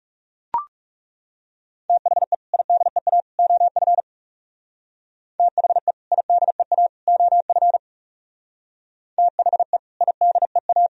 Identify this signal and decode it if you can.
Morse code, heard as a keyed tone, about 36 wpm, 700 Hz. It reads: THE IDEA OF THE IDEA OF THE IDEA
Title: Speed-Racing - Sets of 3 Words 30wpm